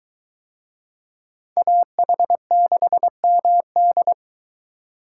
Morse code, heard as a keyed tone, about 21 wpm, 700 Hz. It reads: AH6MD